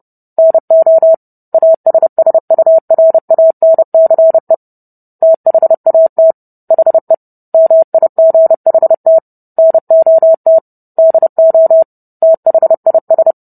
Morse code, heard as a keyed tone, about 30 wpm, 650 Hz. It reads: NO ASSURANCE THAT HE MIGHT NOT DO THIS